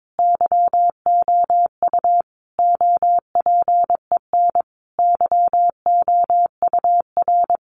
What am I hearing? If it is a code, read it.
Morse code, heard as a keyed tone, about 22 wpm, 700 Hz. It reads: YOU OPEN YOUR
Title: Sentences from Top 1500 Words 22wpm